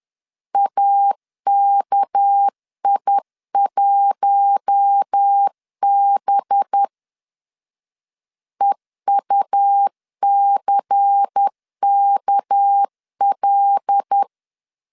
Morse code, heard as a keyed tone, about 10 wpm, 800 Hz. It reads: AKI1B EUCKL
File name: CW / DET / 1 6 1500C S B 10wpm 4137